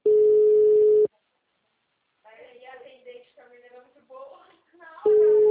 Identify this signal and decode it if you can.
Ringback tone (Europe, 425 Hz), with speech.